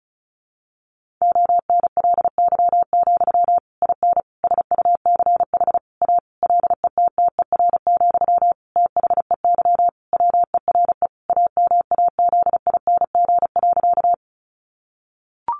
Morse code, heard as a keyed tone, about 34 wpm, 700 Hz. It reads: ONLY, IN SUCH A LETTER, THEY WERE AMAZING.
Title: Sentences from Top 1500 Words 35wpm